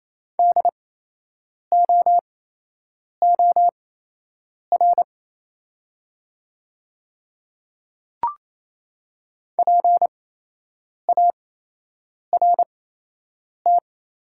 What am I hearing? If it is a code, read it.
Morse code, heard as a keyed tone, about 28 wpm, spaced as for 8 wpm, 700 Hz. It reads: DOOR PART
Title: 8x Character Spacing - Top 500 Words - 28wpm